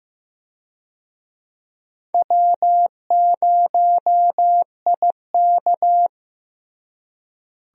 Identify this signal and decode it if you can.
Morse code, heard as a keyed tone, about 15 wpm, 700 Hz. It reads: W0IK
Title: USA Call Signs - 4 Characters - Repeated 3x - 15wpm